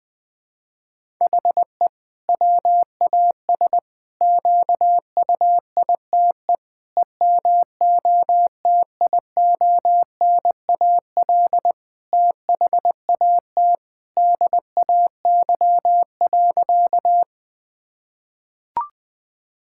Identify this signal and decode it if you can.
Morse code, heard as a keyed tone, about 20 wpm, 700 Hz. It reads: HE WAS QUITE EMOTIONAL THAT DAY.
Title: Sentences from Top 1000 Words - Review Part 5 - 20wpm